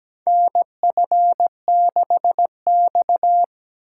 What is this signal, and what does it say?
Morse code, heard as a keyed tone, about 17 wpm, 700 Hz. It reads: NF6X
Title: USA Call Signs - 3x Character Spacing - 17wpm